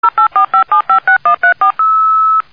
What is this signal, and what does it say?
DTMF, heard as telephone touch tones, tones of about 110 ms. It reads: 0845756134
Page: sonnerie-modem